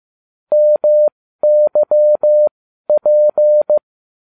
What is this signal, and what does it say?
Morse code, heard as a keyed tone, about 15 wpm, spaced as for 12 wpm, 600 Hz. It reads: MYP
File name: cw / ubacw20 3